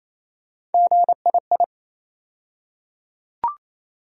Morse code, heard as a keyed tone, about 27 wpm, 700 Hz. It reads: GII
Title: USA Call Sign Postfixes - 3 Letters 28wpm